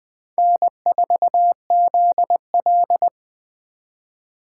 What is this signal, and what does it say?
Morse code, heard as a keyed tone, about 20 wpm, 700 Hz. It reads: N4ZL